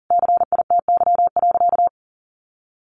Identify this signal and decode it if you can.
Morse code, heard as a keyed tone, about 40 wpm, 700 Hz. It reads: CITY.
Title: Sentences from Top 1200 Words 40wpm